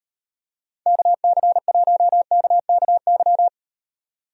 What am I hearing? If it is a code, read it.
Morse code, heard as a keyed tone, about 38 wpm, 700 Hz. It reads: KC1KKY